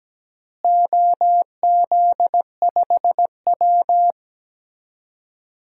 Morse code, heard as a keyed tone, about 17 wpm, 700 Hz. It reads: OZ5W
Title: DXCC-100 Call Signs - 4 Characters - Repeated 3x - 17wpm